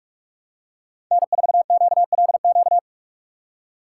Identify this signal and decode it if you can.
Morse code, heard as a keyed tone, about 42 wpm, 700 Hz. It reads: N4QLQ